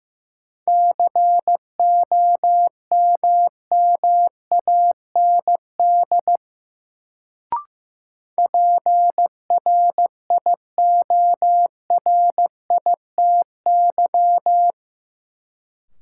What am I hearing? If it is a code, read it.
Morse code, heard as a keyed tone, about 15 wpm, 700 Hz. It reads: COMMAND PRIORITY
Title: Sentences from Top 1600 Words - New Word Drill Alt - 15wpm